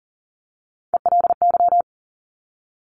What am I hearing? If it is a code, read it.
Morse code, heard as a keyed tone, about 39 wpm, 690 Hz. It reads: ELY
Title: Top Three Letter Combinations in Words 40wpm